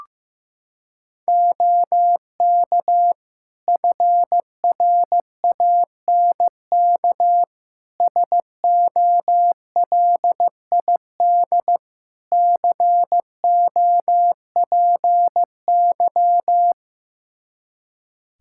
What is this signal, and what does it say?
Morse code, heard as a keyed tone, about 16 wpm, 700 Hz. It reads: OK FRANK SOLID COPY
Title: Practice QSO - Exchange 3 - More Info - 15wpm